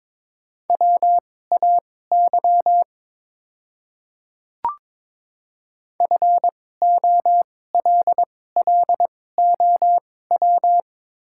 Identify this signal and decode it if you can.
Morse code, heard as a keyed tone, about 22 wpm, spaced as for 16 wpm, 690 Hz. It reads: WAY FOLLOW
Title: 2x Character Spacing - Top 200 Words - 22wpm